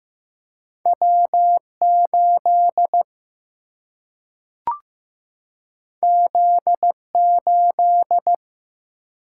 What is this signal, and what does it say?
Morse code, heard as a keyed tone, about 15 wpm, 700 Hz. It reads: W8 Z8
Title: Single-Letter Single-Number 15wpm